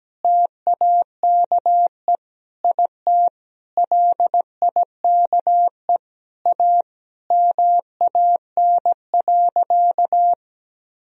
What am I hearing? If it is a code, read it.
Morse code, heard as a keyed tone, about 17 wpm, 700 Hz. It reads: TAKE IT LIKE A MAN.
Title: Sentences from Top 300 Words and Abbreviations 17wpm